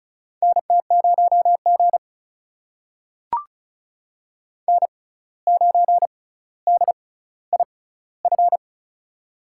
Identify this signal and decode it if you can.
Morse code, heard as a keyed tone, about 36 wpm, spaced as for 17 wpm, 700 Hz. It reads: NT0G N9DIF
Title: USA Call Signs - 6x Character Spacing - 35wpm